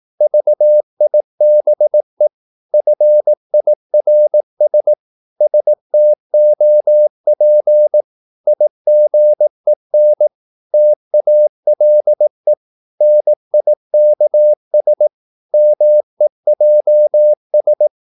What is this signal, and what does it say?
Morse code, heard as a keyed tone, about 18 wpm, 600 Hz. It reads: VIBE FIRS STOP IGEN TALE NIKS MEJS
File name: Korte ord DK 18wpm